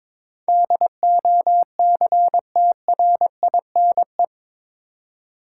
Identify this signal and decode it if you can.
Morse code, heard as a keyed tone, about 22 wpm, 700 Hz. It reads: DOCTRINE
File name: practice / 22wpm / Eight Letter Words - 22wpm